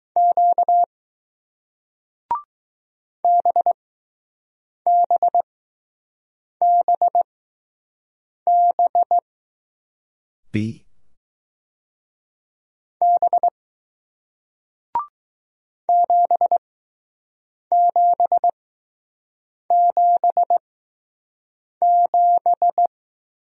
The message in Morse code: Q BBBB B 7777

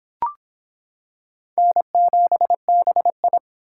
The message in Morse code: N7BI

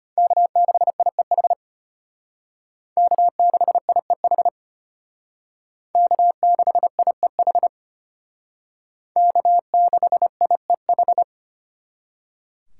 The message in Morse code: K6IEH K6IEH K6IEH K6IEH

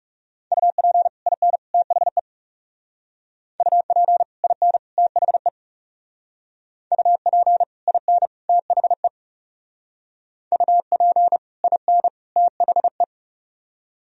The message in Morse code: UP IN THE UP IN THE UP IN THE UP IN THE

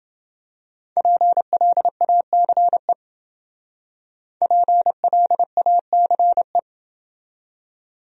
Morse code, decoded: PLACE PLACE

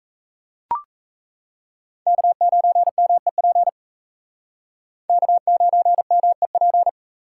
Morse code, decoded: K9MEP K9MEP

